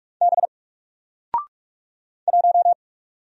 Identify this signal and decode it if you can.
Morse code, heard as a keyed tone, about 44 wpm, 700 Hz. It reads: B 1
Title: Single Letter-Number 45wpm